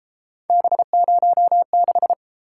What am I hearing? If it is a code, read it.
Morse code, heard as a keyed tone, about 32 wpm, 700 Hz. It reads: B06